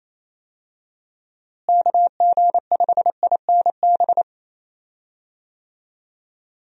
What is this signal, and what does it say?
Morse code, heard as a keyed tone, about 28 wpm, 700 Hz. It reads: KG5INB
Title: USA Call Signs - 6 Characters - Repeated 2x - 28wpm